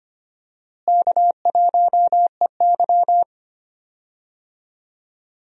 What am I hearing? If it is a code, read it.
Morse code, heard as a keyed tone, about 25 wpm, 700 Hz. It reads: K1EY
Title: USA Call Signs - 4 Characters - Repeated 2x - 25wpm